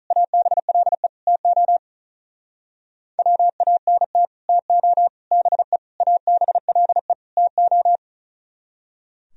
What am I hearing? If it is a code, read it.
Morse code, heard as a keyed tone, about 44 wpm, 700 Hz. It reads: ABLE TO WANT TO BE ABLE TO